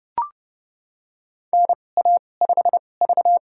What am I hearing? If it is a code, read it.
Morse code, heard as a keyed tone, about 32 wpm, spaced as for 22 wpm, 700 Hz. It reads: NA5V
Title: USA Call Signs - 4 Characters - 2x Character Spacing - 30wpm